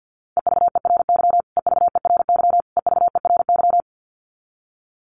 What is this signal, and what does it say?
Morse code, heard as a keyed tone, about 50 wpm, 700 Hz. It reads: EVERY EVERY EVERY
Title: Top 500 Words Repeated 50wpm